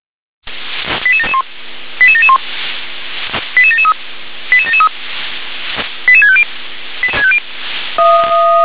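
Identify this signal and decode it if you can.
Five-tone selective call (ZVEI), heard as quick runs of five tones, tones of about 70 ms. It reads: 88771 88771 88773 88773 88650
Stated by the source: ZVEI five-tone selective call 88702 received via SDR.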